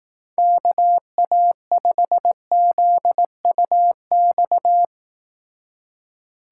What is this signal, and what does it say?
Morse code, heard as a keyed tone, about 18 wpm, 700 Hz. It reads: KA5ZUX
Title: Speed Racing - USA 2x3 Call Signs 15wpm